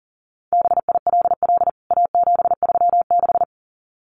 Morse code, heard as a keyed tone, about 40 wpm, 700 Hz. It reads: BILL A736
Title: CWT Exchange 40wpm